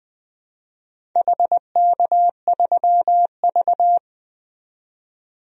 Morse code, heard as a keyed tone, about 20 wpm, 700 Hz. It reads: HK3V
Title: DXCC-100 Call Signs - 4 Characters - 20wpm